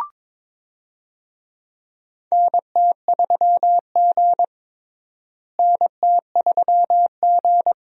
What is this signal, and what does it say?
Morse code, heard as a keyed tone, about 22 wpm, 700 Hz. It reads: NT3G NT3G
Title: USA Call Signs - 4 Characters - Repeated 2x - 22wpm